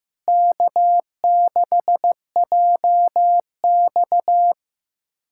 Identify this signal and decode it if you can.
Morse code, heard as a keyed tone, about 15 wpm, 700 Hz. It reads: K6JX
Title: USA Callsign DE Callsign 15wpm